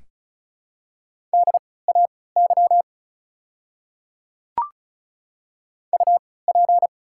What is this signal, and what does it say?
Morse code, heard as a keyed tone, about 36 wpm, spaced as for 20 wpm, 690 Hz. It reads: DAY UP
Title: 3x Character Spacing - Top 100 Words - 35wpm